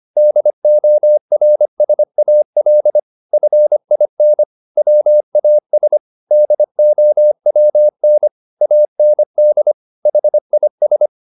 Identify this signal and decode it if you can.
Morse code, heard as a keyed tone, about 25 wpm, 600 Hz. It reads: DORSAL FIN WAS DOWN AND HIS